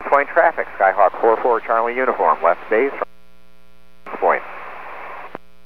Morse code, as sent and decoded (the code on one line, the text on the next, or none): none